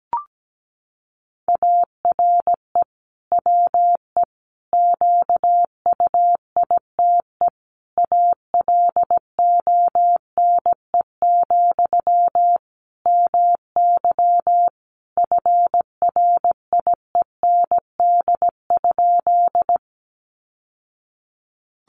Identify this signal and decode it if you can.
Morse code, heard as a keyed tone, about 17 wpm, 700 Hz. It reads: ARE WE QUITE ALONE, MY FRIEND?
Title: Sentences from Top 1000 Words 17wpm